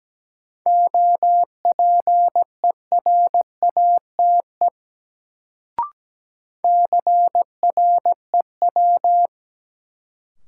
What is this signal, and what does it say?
Morse code, heard as a keyed tone, about 17 wpm, 700 Hz. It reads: OPERATE CREW
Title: Sentences from Top 1500 Words - New Word Drill - 17wpm